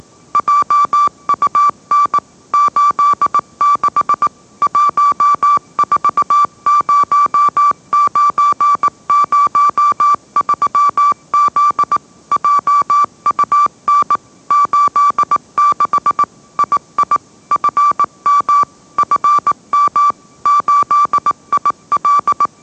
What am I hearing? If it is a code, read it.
Morse code, heard as a keyed tone, about 18 wpm, 1160 Hz. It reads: JUN 86 140903Z JUN 86 II FM FM 8IL